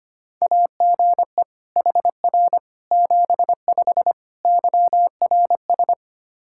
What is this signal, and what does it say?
Morse code, heard as a keyed tone, about 25 wpm, 700 Hz. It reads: AGE HR 75 YRS
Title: K1USN SST QSO Practice 25wpm